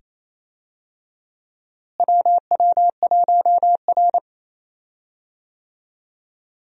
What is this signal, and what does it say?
Morse code, heard as a keyed tone, about 28 wpm, 700 Hz. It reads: WW1R
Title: USA Call Signs - 4 Characters - Repeated 2x - 28wpm